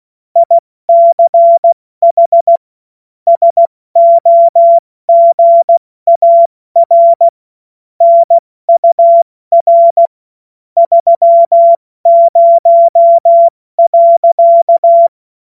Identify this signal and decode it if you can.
Morse code, heard as a keyed tone, about 16 wpm, 680 Hz. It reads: ICH SOGAR NUR 30.
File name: Morsekurs / Witze / Witz 06. September 2022 70 F80